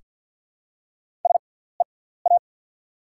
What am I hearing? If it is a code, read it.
Morse code, heard as a keyed tone, about 48 wpm, spaced as for 17 wpm, 690 Hz. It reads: SEA
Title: 6x Character Spacing - Top 300 Words - 50wpm